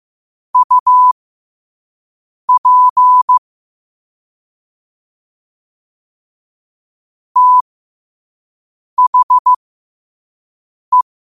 Morse code, heard as a keyed tone, about 15 wpm, spaced as for 5 wpm, 1000 Hz. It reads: UP THE
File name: qotdfiles / 5wpm